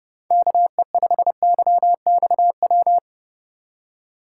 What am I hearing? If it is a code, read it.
Morse code, heard as a keyed tone, about 30 wpm, 700 Hz. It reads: KE5YXW